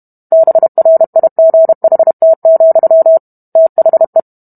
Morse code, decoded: BRIGHT, THE

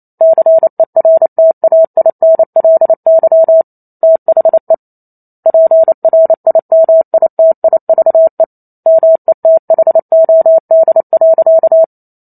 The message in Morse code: CERTAINLY THE PRIMITIVE METHOD.

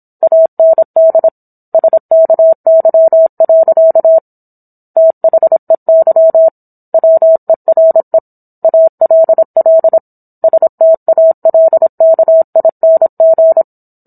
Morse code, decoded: AND SKY. THEY WERE ALL STALKING

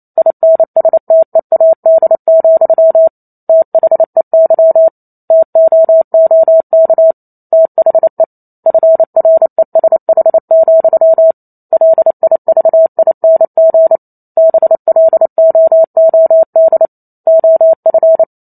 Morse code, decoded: INSTEAD, THEY TOOK THE FRESH, LIVING BLOOD OF

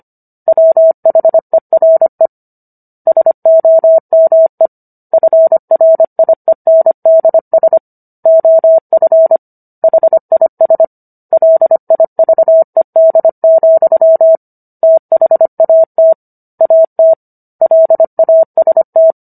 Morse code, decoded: WHERE SOME FRIENDS OF HIS LIVED, THAT AT LAST

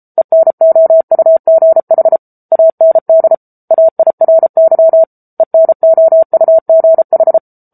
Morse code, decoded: ENOUGH AND AIRY ENOUGH

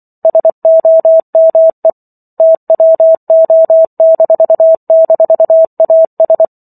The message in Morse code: SOME TWO--AS

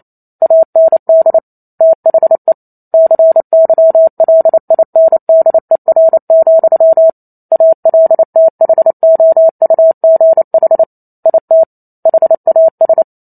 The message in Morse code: AND THE CYLINDER, ALTHOUGH IT HAS

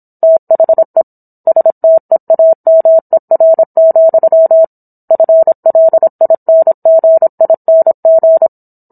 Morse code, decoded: THE STEAMER, FLINGING

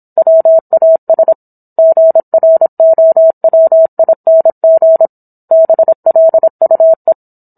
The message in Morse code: WAS GROWING BLUE

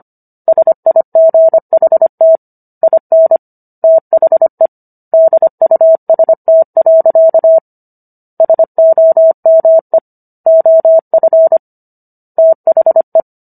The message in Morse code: SIGHT IN THE DUST. SOME OF THE

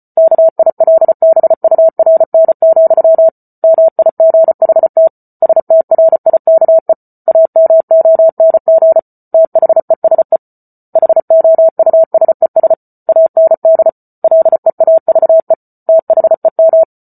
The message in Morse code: KILBURN, MIGHT STRIKE AMONG THESE HOUSES AND LEAVE THEM

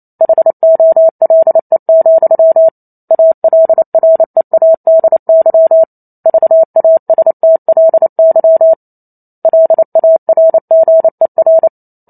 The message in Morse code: HOLE, ALREADY VASTLY LARGER

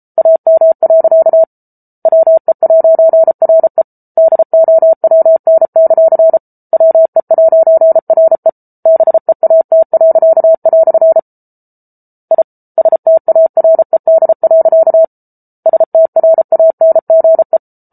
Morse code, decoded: AM. WE'RE DOWN; WE'RE BEAT." I STARED. STRANGE

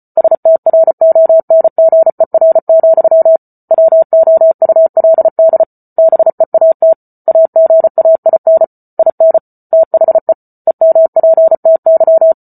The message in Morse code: STRONGER, WOULD BEAT AGAIN IN THE EMPTY

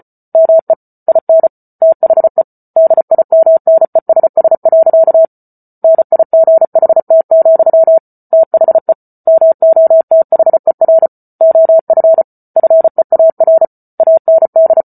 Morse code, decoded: ME IN THE DIMNESS. NIGHT, THE MOTHER OF FEAR AND

